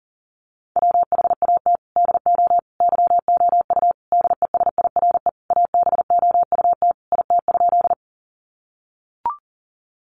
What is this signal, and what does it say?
Morse code, heard as a keyed tone, about 40 wpm, 700 Hz. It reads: WHAT DO YOU DESIRE ABOUT IT?
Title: Sentences from Top 1100 Words 40wpm